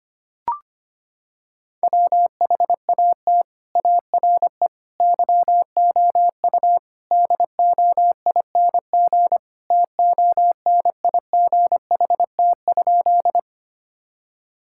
Morse code, heard as a keyed tone, about 25 wpm, 700 Hz. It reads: WHAT ARE YOU DOING TONIGHT?
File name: practice / 25wpm / Sentences with 4-Word Phrases - 25wpm